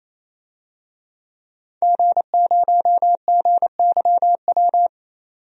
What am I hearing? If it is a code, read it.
Morse code, heard as a keyed tone, about 28 wpm, 700 Hz. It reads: G0GYW
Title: DXCC-100 Call Signs - 5 Characters - 28wpm